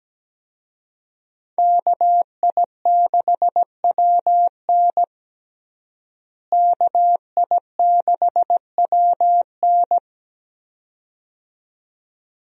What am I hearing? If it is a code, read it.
Morse code, heard as a keyed tone, about 17 wpm, 700 Hz. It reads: KI6WN KI6WN
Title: USA Call Signs - 5 Characters - Repeated 2x - 17wpm